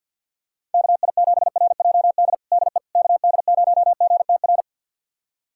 Morse code, heard as a keyed tone, about 52 wpm, 700 Hz. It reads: KI7RJD DE KD0GTR
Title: USA Callsign DE Callsign - 6 Characters - 50wpm